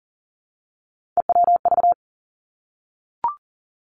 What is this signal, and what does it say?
Morse code, heard as a keyed tone, about 41 wpm, 700 Hz. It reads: EWV
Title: Three Challenging Letter-Number Combinations 40wpm